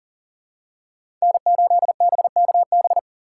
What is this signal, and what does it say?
Morse code, heard as a keyed tone, about 40 wpm, 700 Hz. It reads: N8BKB
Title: USA Call Signs - 5 Characters - 4x Character Spacing - 40wpm